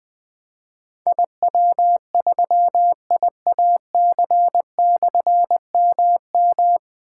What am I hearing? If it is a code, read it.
Morse code, heard as a keyed tone, about 21 wpm, 700 Hz. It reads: IW3IAC/MM
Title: Difficult Call Signs 20wpm